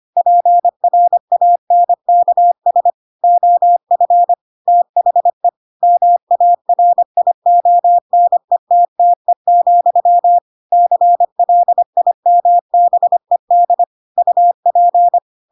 Morse code, heard as a keyed tone, about 25 wpm, 700 Hz. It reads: PRANKS OF THE MARIONETTE, CLIMBED UP
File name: Pinocchio / Pinocchio ch7 25 25 ik1hge